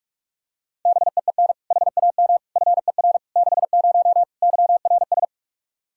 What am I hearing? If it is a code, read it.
Morse code, heard as a keyed tone, about 46 wpm, 700 Hz. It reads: BEEN HAM FER 60 YRS
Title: Practice QSO - Exchange 3 - More Info - 45wpm